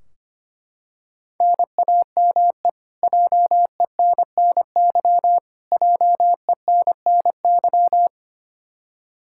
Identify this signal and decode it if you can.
Morse code, heard as a keyed tone, about 25 wpm, 700 Hz. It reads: NAME JENNY JENNY
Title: Practice QSO - Exchange 1 - Beginning - 25wpm